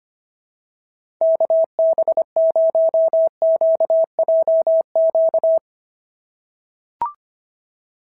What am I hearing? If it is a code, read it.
Morse code, heard as a keyed tone, about 25 wpm, 690 Hz. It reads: KB0QJQ
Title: SOTA QSO Practice 25wpm